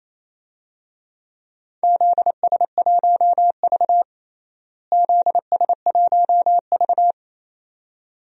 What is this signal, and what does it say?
Morse code, heard as a keyed tone, about 28 wpm, 700 Hz. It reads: ZS1V ZS1V
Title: DXCC-100 Call Signs - 4 Characters - Repeated 2x - 28wpm